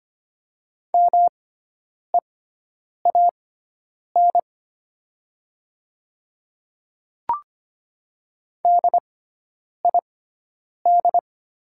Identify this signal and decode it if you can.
Morse code, heard as a keyed tone, about 25 wpm, spaced as for 9 wpm, 690 Hz. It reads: MEAN DID
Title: 6x Character Spacing - Top 200 Words - 25wpm